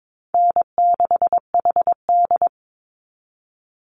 Morse code, heard as a keyed tone, about 22 wpm, 690 Hz. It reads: N6HD
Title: Call Signs Repeated 3x - 22wpm